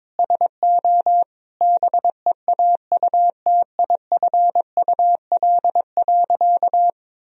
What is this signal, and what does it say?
Morse code, heard as a keyed tone, about 22 wpm, 700 Hz. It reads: SO BEAUTIFUL.
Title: Sentences from Top 1000 Words - Review Part 3 - 22wpm